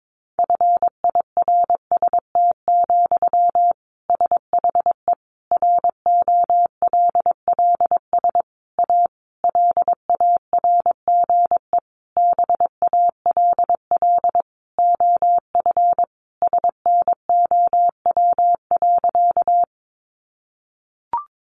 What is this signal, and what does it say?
Morse code, heard as a keyed tone, about 22 wpm, 700 Hz. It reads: FIRST, SHE ROLLS A LARGE BALL OF SNOW.
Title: Easy Phrases 22wpm